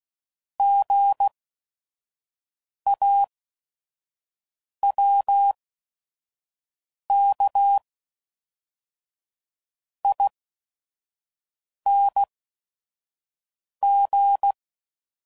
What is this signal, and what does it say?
Morse code, heard as a keyed tone, about 16 wpm, spaced as for 5 wpm, 800 Hz. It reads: GAWKING
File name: Prefixes / By First Letter / Gs